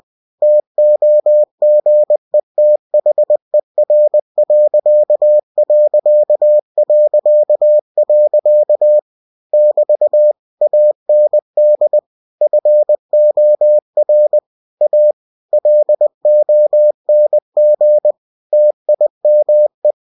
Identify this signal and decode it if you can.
Morse code, heard as a keyed tone, about 20 wpm, 600 Hz. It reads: TOGETHER.... <BT> AND FOR A LONG TIME
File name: cw-books / winnie-the-pooh / winnie-20wpm-page-0019